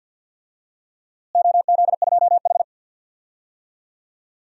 Morse code, heard as a keyed tone, about 52 wpm, 700 Hz. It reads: OZ2H
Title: DXCC-100 Call Signs - 4 Characters - Repeated 3x - 50wpm